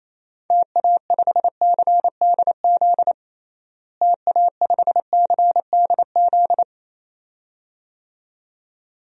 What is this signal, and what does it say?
Morse code, heard as a keyed tone, about 28 wpm, 700 Hz. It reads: TA5CDZ TA5CDZ
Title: DXCC-100 Call Signs - 6 Characters - Repeated 2x - 28wpm